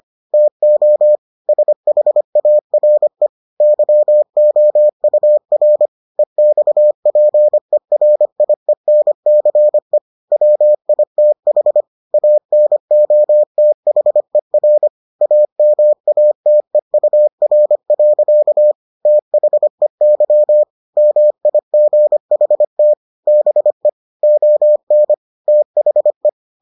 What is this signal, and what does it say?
Morse code, heard as a keyed tone, about 25 wpm, 600 Hz. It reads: TO SHARE YOUR EXPERIENCE WITH ANOTHER AMATEUR. THEY MIGHT BE ON THE